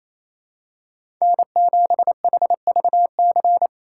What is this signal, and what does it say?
Morse code, heard as a keyed tone, about 27 wpm, 700 Hz. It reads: N7HVC
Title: USA Call Signs - 4x Character Spacing - 28wpm